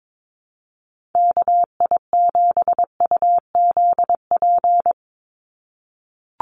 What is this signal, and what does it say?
Morse code, heard as a keyed tone, about 22 wpm, 700 Hz. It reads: KI7UZP
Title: USA Call Signs 22wpm